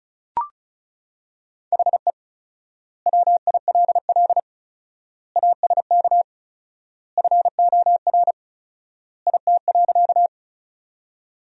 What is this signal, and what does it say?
Morse code, heard as a keyed tone, about 36 wpm, 700 Hz. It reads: HE WILL ASK FOR IT.
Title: 4x Word-Spacing - Sentences from Top 200 Words - 35wpm